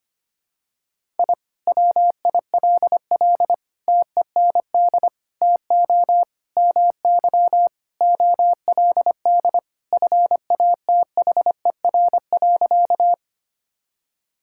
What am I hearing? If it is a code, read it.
Morse code, heard as a keyed tone, about 25 wpm, 700 Hz. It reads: I WILL TEND TO MY OLD FATHER.
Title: Sentences from Top 1000 Words - Review Part 5 - 25wpm